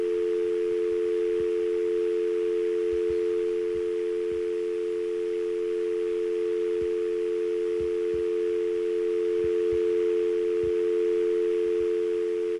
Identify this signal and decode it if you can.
Dial tone (North America), 350 + 440 Hz.
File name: dial-tone-sfx